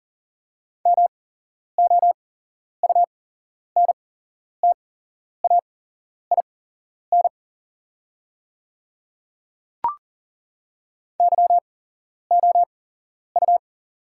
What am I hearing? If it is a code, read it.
Morse code, heard as a keyed tone, about 38 wpm, spaced as for 11 wpm, 690 Hz. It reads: MOUNTAIN YOU
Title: 8x Character Spacing - Top 300 Words - 40wpm